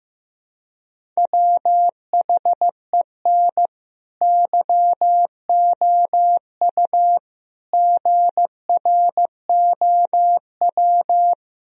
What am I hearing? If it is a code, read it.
Morse code, heard as a keyed tone, about 15 wpm, 700 Hz. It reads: WHEN YOU GROW
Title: Sentences from Top 1000 Words - Review Part 1 - 15wpm